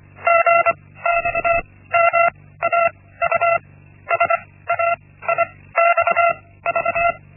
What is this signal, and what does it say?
Morse code, heard as a keyed tone, about 24 wpm, spaced as for 16 wpm, 680 Hz. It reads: GXMAU SAIXV
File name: Linksmakalnis-1